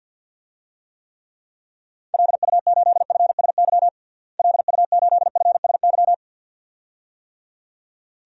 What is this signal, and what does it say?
Morse code, heard as a keyed tone, about 52 wpm, 700 Hz. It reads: LU8FSY LU8FSY
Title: DXCC-100 Call Signs - 6 Characters - Repeated 2x - 50wpm